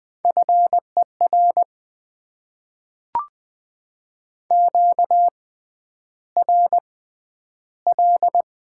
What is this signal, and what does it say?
Morse code, heard as a keyed tone, about 20 wpm, spaced as for 10 wpm, 690 Hz. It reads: FER QRL
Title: Top 20 QSO Elements - 6x Character Spacing - 20wpm